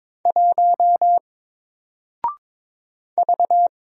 Morse code, heard as a keyed tone, about 22 wpm, 700 Hz. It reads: 1 V
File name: Single-Letter-Number-22wpm